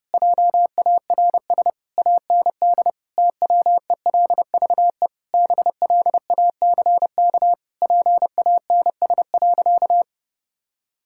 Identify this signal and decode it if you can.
Morse code, heard as a keyed tone, about 32 wpm, 700 Hz. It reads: JARS AND TWELVE BLACK PANS.
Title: Pangram Challenge v2 - 30wpm